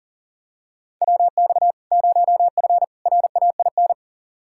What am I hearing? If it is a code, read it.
Morse code, heard as a keyed tone, about 40 wpm, 700 Hz. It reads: WX 0F RAIN
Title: Practice QSO - Exchange 2 - Info - 40wpm